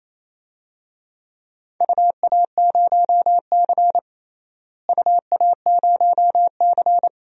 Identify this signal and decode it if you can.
Morse code, heard as a keyed tone, about 28 wpm, 700 Hz. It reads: UA0C UA0C
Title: DXCC-100 Call Signs - 4 Characters - Repeated 3x - 28wpm